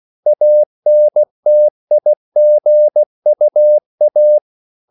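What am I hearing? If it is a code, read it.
Morse code, heard as a keyed tone, about 16 wpm, 600 Hz. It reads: ANTIGUA
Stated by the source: CW practice of national capitals at 16 words per minute, from ON6ZQ.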